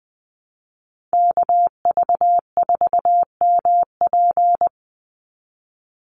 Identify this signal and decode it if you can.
Morse code, heard as a keyed tone, about 20 wpm, 700 Hz. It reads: KV4MP
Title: USA Call Signs 20wpm